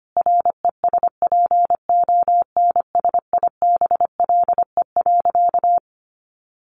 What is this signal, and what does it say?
Morse code, heard as a keyed tone, about 25 wpm, 700 Hz. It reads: RESPONSIBLE.
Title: A Sound Of Thunder 25wpm